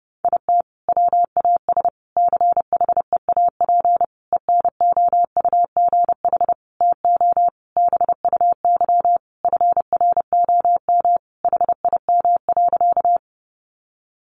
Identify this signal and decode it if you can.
Morse code, heard as a keyed tone, about 30 wpm, 700 Hz. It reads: IT WAS CHEAP ENOUGH TO BUY FROM HIM.
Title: Sentences from Top 1200 Words 30wpm